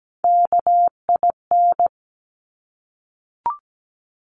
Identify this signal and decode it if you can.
Morse code, heard as a keyed tone, about 17 wpm, 700 Hz. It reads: KIN
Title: Top Three Letter Combinations in Words 17wpm